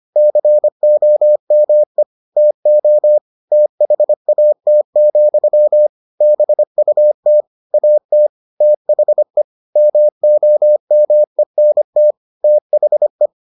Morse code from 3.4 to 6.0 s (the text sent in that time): THAT,